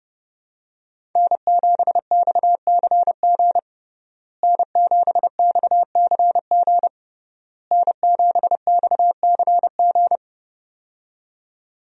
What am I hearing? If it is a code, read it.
Morse code, heard as a keyed tone, about 32 wpm, 700 Hz. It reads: N7XCG N7XCG N7XCG
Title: USA Call Signs - 5 Characters - Repeated 3x - 30wpm